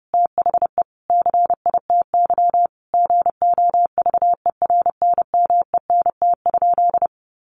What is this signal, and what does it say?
Morse code, heard as a keyed tone, about 30 wpm, 700 Hz. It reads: THE CITY GOVERNMENT?
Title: Sentences from Top 1400 Words 30wpm